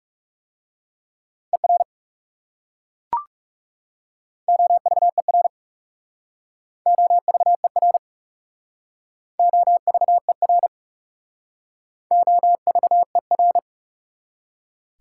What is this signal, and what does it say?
Morse code, heard as a keyed tone, about 36 wpm, 690 Hz. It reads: ER OVER OVER OVER OVER
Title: Speed-Racing Prefixes and Suffixes 30wpm